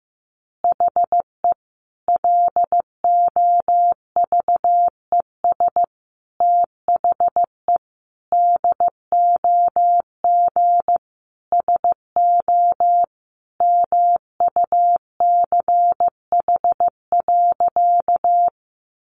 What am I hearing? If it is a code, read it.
Morse code, heard as a keyed tone, about 15 wpm, 700 Hz. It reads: HE LOVES THE DOG SO MUCH.
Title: Additional Easy Phrases 15wpm